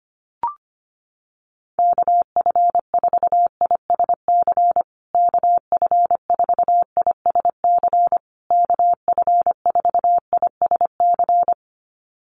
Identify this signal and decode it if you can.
Morse code, heard as a keyed tone, about 25 wpm, 700 Hz. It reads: KF4ISC KF4ISC KF4ISC
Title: USA Call Signs Repeated 3x 25wpm